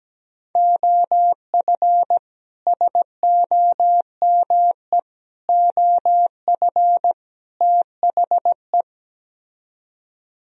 Morse code, heard as a keyed tone, about 17 wpm, 700 Hz. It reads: OF SOME OF THE
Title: Speed-Racing - Sets of 4 Words 17wpm